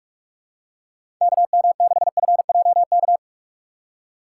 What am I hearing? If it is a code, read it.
Morse code, heard as a keyed tone, about 47 wpm, 700 Hz. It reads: KM6FJK
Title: USA Call Signs - 6x Character Spacing - 45wpm